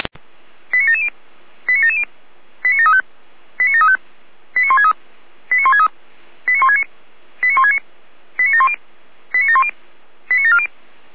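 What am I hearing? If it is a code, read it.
Five-tone selective call (ZVEI), heard as quick runs of five tones, tones of about 70 ms. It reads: 89770 89770 89735 89735 89163 89163 89169 89169 89710 89710 89740
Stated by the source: ZVEI five-tone call received on SDR.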